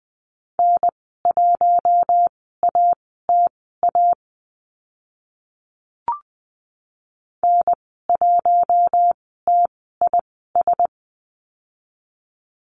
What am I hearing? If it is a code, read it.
Morse code, heard as a keyed tone, about 20 wpm, spaced as for 14 wpm, 700 Hz. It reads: N1ATA N1TIS
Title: 042 Lesson 8g - Number 1 - Call Signs 20wpm